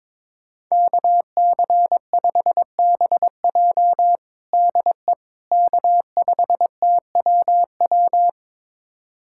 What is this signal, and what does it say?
Morse code, heard as a keyed tone, about 22 wpm, 700 Hz. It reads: KC5BJ DE K5TWW
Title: USA Callsign DE Callsign - 5 Characters - 22wpm